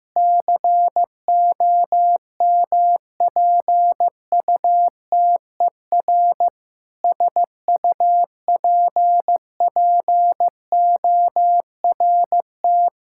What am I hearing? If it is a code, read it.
Morse code, heard as a keyed tone, about 15 wpm, 700 Hz. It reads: COMPUTER SUPPORT
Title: Practice QSO - Exchange 3 - More Info - 15wpm